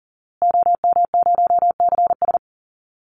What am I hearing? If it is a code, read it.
Morse code, heard as a keyed tone, about 40 wpm, 700 Hz. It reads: OM0CS
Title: Call Signs Repeated 2x - 40wpm